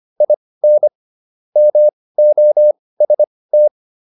Morse code, heard as a keyed tone, about 25 wpm, spaced as for 18 wpm, 600 Hz. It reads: IN MOST